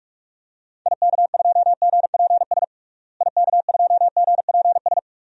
Morse code, heard as a keyed tone, about 46 wpm, 690 Hz. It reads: IK2GPS IK2GPS
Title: DXCC-100 Call Signs - 6 Characters - Repeated 2x - 45wpm